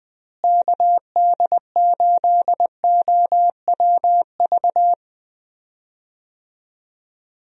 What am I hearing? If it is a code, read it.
Morse code, heard as a keyed tone, about 21 wpm, 700 Hz. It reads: KD8OWV
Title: USA Call Signs - 6 Characters - Repeated 2x - 20wpm